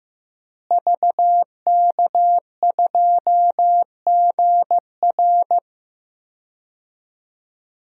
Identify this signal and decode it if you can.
Morse code, heard as a keyed tone, about 15 wpm, 700 Hz. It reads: VK2GR
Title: DXCC-100 Call Signs - 5 Characters - Repeated 3x - 15wpm